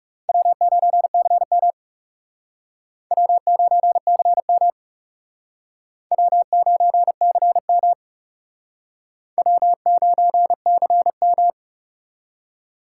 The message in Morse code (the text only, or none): W9CM W9CM W9CM W9CM